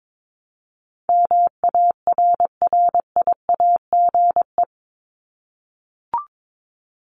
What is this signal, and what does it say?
Morse code, heard as a keyed tone, about 22 wpm, 700 Hz. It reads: MARRIAGE
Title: Top 751-1000 Words 22wpm